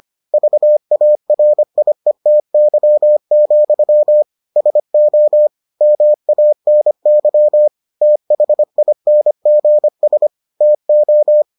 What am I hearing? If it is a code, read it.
Morse code, heard as a keyed tone, about 25 wpm, 600 Hz. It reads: VARIETY, SO MANY THINGS TO